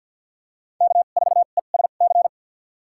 Morse code, heard as a keyed tone, about 53 wpm, spaced as for 37 wpm, 700 Hz. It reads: K4ESC
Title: USA Call Signs - 5 Characters - 2x Character Spacing - 50wpm